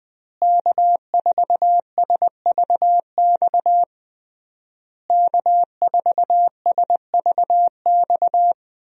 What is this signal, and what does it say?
Morse code, heard as a keyed tone, about 20 wpm, 700 Hz. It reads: K4SVX K4SVX
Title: USA Call Signs - 5 Characters - Repeated 3x - 20wpm